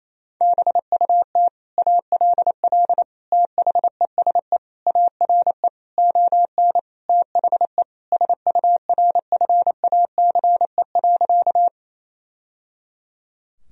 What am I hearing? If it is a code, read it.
Morse code, heard as a keyed tone, about 28 wpm, 700 Hz. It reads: BUT ALL THESE ARE ON THE SURFACE.
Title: Sentences from Top 1000 Words - Review Part 3 - 28wpm